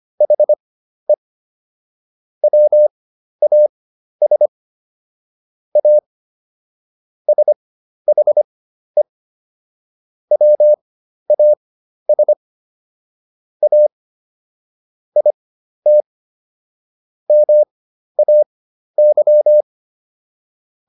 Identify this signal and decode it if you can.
Morse code, heard as a keyed tone, about 25 wpm, spaced as for 12 wpm, 600 Hz. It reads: HE WAS A SHE WAS A IT MAY